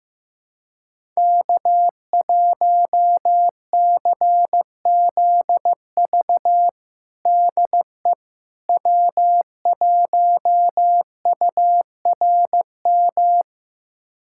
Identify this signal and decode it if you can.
Morse code, heard as a keyed tone, about 16 wpm, 700 Hz. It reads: K1CZV DE W1URM
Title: USA Callsign DE Callsign - 5 Characters - 15wpm